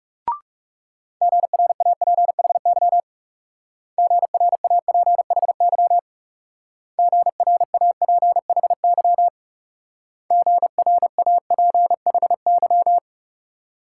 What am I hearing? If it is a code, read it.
Morse code, heard as a keyed tone, about 39 wpm, 700 Hz. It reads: GRAPHY GRAPHY GRAPHY GRAPHY